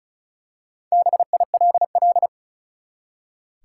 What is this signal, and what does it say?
Morse code, heard as a keyed tone, about 34 wpm, 700 Hz. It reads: BILL
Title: Four Letter Words - 35wpm